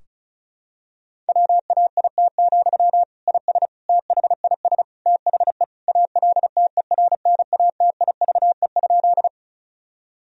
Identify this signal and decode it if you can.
Morse code, heard as a keyed tone, about 34 wpm, 700 Hz. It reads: WAIT, IS THIS THE ALTERNATIVE?
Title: Sentences from Top 1000 Words - Review Part 4 - 35wpm